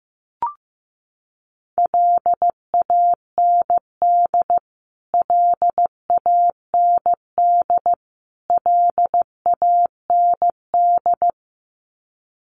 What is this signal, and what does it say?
Morse code, heard as a keyed tone, about 15 wpm, 700 Hz. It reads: LAND LAND LAND
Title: Top 500 Words Repeated 15wpm